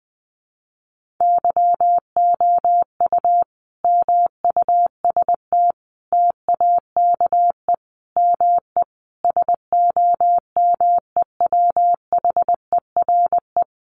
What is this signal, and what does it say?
Morse code, heard as a keyed tone, about 20 wpm, 700 Hz. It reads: YOU MUST TAKE ME SOMEWHERE
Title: Sentences from Top 1400 Words 20wpm